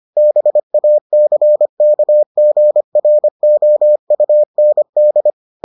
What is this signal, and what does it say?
Morse code, heard as a keyed tone, about 25 wpm, 600 Hz. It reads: BACKGROUND